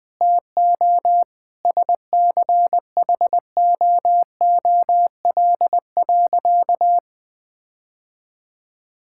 Morse code, heard as a keyed tone, about 20 wpm, 700 Hz. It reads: TO SCHOOL.
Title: Sentences with 4-Word Phrases - Spoken Beginning - 20wpm